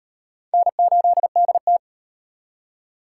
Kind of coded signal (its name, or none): Morse code